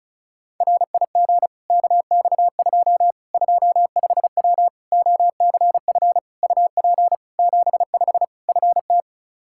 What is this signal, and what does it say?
Morse code, heard as a keyed tone, about 36 wpm, 700 Hz. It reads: RIG KX2 25W OCF UP 75 FT
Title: K1USN SST QSO Practice 35wpm